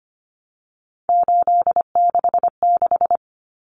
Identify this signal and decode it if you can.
Morse code, heard as a keyed tone, about 25 wpm, 700 Hz. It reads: 866
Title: Common Cut Numbers - 3-Digit 25wpm